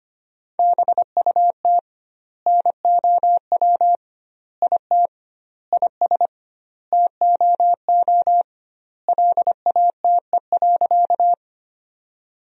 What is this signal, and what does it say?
Morse code, heard as a keyed tone, about 25 wpm, 700 Hz. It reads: BUT NOW IT IS TOO LATE.
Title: 2x Word-Spacing - Sentences from Top 300 Words - 25wpm